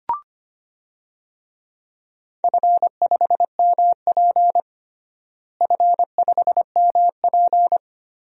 Morse code, heard as a keyed tone, about 25 wpm, 700 Hz. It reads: F5MP F5MP
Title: DXCC-100 Call Signs - 4 Characters - Repeated 2x - 25wpm